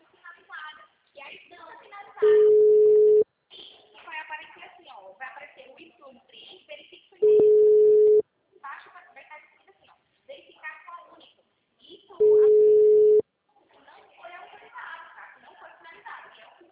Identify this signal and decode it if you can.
Ringback tone (Europe, 425 Hz), with speech.
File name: out- -2021- - - .45393